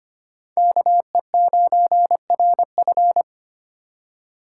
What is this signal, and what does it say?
Morse code, heard as a keyed tone, about 25 wpm, 700 Hz. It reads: KE9RF
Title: USA Call Signs - 3x Character Spacing - 25wpm